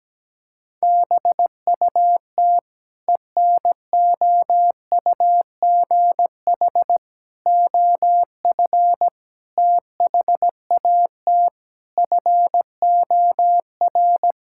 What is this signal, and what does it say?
Morse code, heard as a keyed tone, about 17 wpm, 700 Hz. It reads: BUT ENOUGH OF THAT FOR
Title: Sentences from Top 1000 Words - Review Part 5 - 17wpm